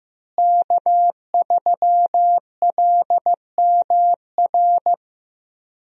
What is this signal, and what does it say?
Morse code, heard as a keyed tone, about 15 wpm, 700 Hz. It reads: K3LMR
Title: USA Call Signs - 5 Characters - Repeated 2x - 15wpm